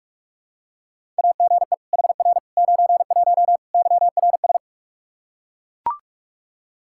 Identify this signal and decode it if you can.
Morse code, heard as a keyed tone, about 41 wpm, 700 Hz. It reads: AGE HR 91 YRS
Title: K1USN SST QSO Practice 45wpm